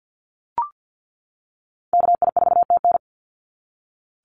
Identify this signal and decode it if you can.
Morse code, heard as a keyed tone, about 48 wpm, 700 Hz. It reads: KI4TN
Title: NA SOTA Activator Callsigns 50wpm